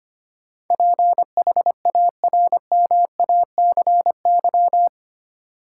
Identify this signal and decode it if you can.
Morse code, heard as a keyed tone, about 25 wpm, 700 Hz. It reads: PHARMACY
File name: Eight Letter Words - 25wpm